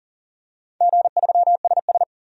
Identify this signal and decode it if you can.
Morse code, heard as a keyed tone, about 40 wpm, 700 Hz. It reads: G3SS
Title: DXCC-100 Call Signs - 4 Characters - 40wpm